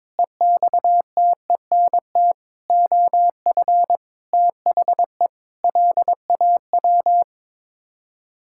Morse code, heard as a keyed tone, about 22 wpm, 700 Hz. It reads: EXTENT OF THE LAW
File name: practice / 22wpm / Sets of 7 Words - 22wpm